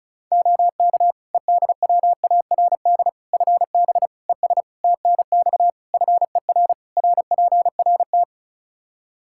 Morse code, heard as a keyed tone, about 36 wpm, 700 Hz. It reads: OK EDWARD FB ES TNX FER RPRT
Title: Practice QSO - Exchange 2 - Info - 35wpm